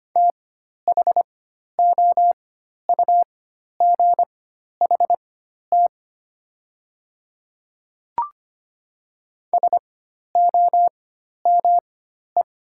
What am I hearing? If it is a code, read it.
Morse code, heard as a keyed tone, about 25 wpm, spaced as for 12 wpm, 700 Hz. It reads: THOUGHT SOME
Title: 4x Character Spacing - Top 300 Words - 25wpm